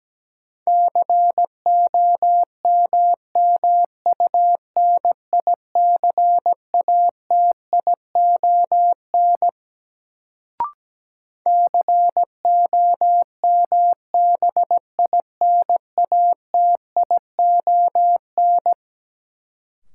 Morse code, heard as a keyed tone, about 17 wpm, 700 Hz. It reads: COMMUNICATION COMBINATION
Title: Sentences from Top 1000 Words - Review Part 6 - New Word Drill - 17wpm